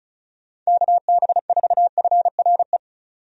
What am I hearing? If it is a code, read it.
Morse code, heard as a keyed tone, about 36 wpm, 700 Hz. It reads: KB4FRE
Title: USA Call Signs - 6 Characters - 35wpm